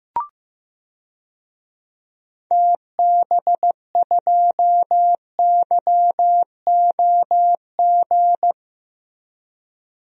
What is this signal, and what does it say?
Morse code, heard as a keyed tone, about 15 wpm, 700 Hz. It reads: TB2YOG
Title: DXCC-100 Call Signs - 6 Characters - Repeated 2x - 15wpm